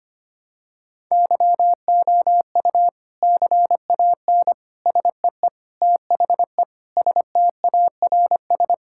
Morse code, heard as a keyed tone, about 25 wpm, 700 Hz. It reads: YOU CAN SEE THE STARS
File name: practice / 25wpm / Sentences with 3-Word Phrases - 25wpm